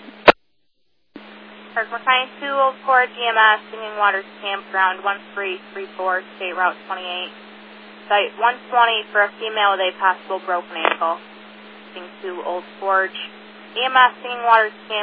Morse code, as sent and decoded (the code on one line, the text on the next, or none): none